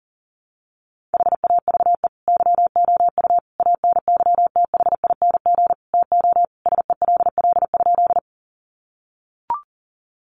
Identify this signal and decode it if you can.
Morse code, heard as a keyed tone, about 40 wpm, 700 Hz. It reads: HAVE YOU ANYTHING TO SELL?
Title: Sentences from Top 1100 Words 40wpm